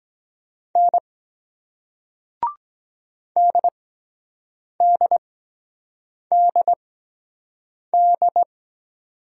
Morse code, heard as a keyed tone, about 23 wpm, spaced as for 7 wpm, 700 Hz. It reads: N DDDD